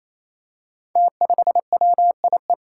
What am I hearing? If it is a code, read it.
Morse code, heard as a keyed tone, about 28 wpm, 700 Hz. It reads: T5WIE
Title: Five Challenging Letter-Number Combinations 28wpm